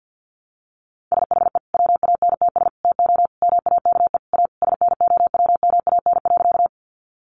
Morse code, heard as a keyed tone, about 50 wpm, 700 Hz. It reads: SHE WANTS TO MAKE A SNOWMAN.
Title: Easy Phrases 50wpm